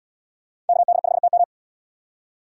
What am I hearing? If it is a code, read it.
Morse code, heard as a keyed tone, about 71 wpm, 700 Hz. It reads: KD5TM